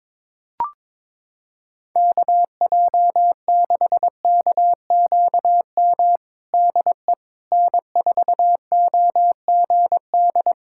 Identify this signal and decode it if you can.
Morse code, heard as a keyed tone, about 22 wpm, 700 Hz. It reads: KJ6KQM DE N4OGD
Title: Practice QSO - Exchange 4 - Ending - 22wpm